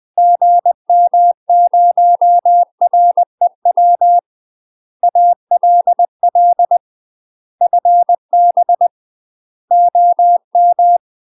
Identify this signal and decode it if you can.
Morse code, heard as a keyed tone, about 20 wpm, 700 Hz. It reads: GM0REW ALL FB OM
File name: UK MASTER QSOs TRAINING / 20WPM / UK MASTER QSOs 0049